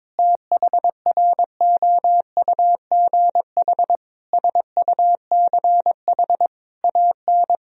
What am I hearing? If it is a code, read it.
Morse code, heard as a keyed tone, about 22 wpm, 700 Hz. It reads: THROUGH SUCH AN
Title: Sentences from Top 1000 Words - Review Part 3 - 22wpm